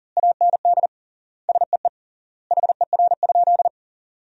Morse code, heard as a keyed tone, about 40 wpm, 700 Hz. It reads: AND SEE HER?
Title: 3x Word-Spacing - Sentences from Top 100 Words - 40wpm